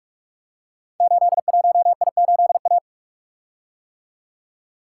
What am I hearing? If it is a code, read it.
Morse code, heard as a keyed tone, about 45 wpm, 700 Hz. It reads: 81I8A
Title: Code Groups - 5 Letter:Numbers - 45wpm